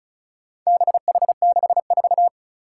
Morse code, heard as a keyed tone, about 34 wpm, 700 Hz. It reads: BH64
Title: Four Challenging Letter-Number Combinations 35wpm